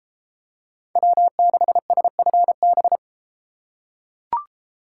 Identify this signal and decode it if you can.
Morse code, heard as a keyed tone, about 31 wpm, 700 Hz. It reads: W6SFB